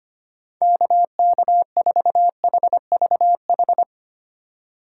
Morse code, heard as a keyed tone, about 25 wpm, 700 Hz. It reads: KK4HVH
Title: USA Call Signs - 6x Character Spacing - 25wpm